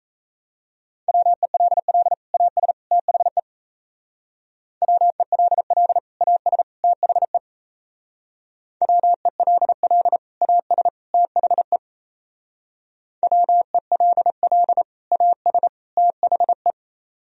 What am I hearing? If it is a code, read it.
Morse code, heard as a keyed tone, about 37 wpm, 700 Hz. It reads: WELL AS THE WELL AS THE WELL AS THE WELL AS THE